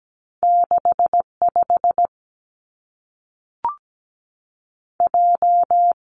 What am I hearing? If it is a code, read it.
Morse code, heard as a keyed tone, about 17 wpm, 700 Hz. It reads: 65 J
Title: Two Challenging Letter-Number Combinations 17wpm